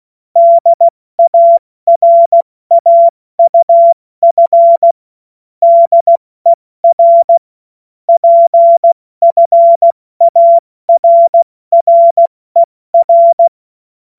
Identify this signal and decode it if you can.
Morse code, heard as a keyed tone, about 16 wpm, spaced as for 14 wpm, 680 Hz. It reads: DARAUF DER PFARRER
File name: Morsekurs / Witze / Witz 17.Oktober 2023 70 F80